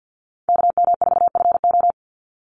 Morse code, heard as a keyed tone, about 50 wpm, 700 Hz. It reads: KM4FO
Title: Call Signs - 50wpm